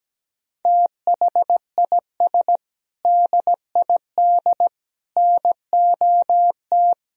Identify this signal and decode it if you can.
Morse code, heard as a keyed tone, about 17 wpm, 700 Hz. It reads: THIS DID NOT
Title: Sentences from Top 1000 Words - Review Part 2 - 17wpm